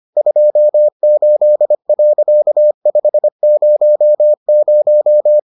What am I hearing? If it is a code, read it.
Morse code, heard as a keyed tone, about 25 wpm, 600 Hz. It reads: 28.500